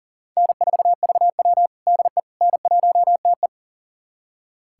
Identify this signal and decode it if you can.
Morse code, heard as a keyed tone, about 40 wpm, 700 Hz. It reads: N4VW DE N1TE
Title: USA Callsign DE Callsign - 4 Characters - 40wpm